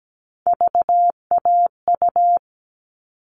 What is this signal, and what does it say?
Morse code, heard as a keyed tone, about 17 wpm, 700 Hz. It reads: VAU
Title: All Cut Numbers - 3-Digit 17wpm